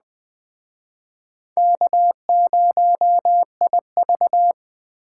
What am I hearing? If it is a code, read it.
Morse code, heard as a keyed tone, about 20 wpm, 700 Hz. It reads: K0IV
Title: Speed Racing - USA 1x2 Call Signs 17wpm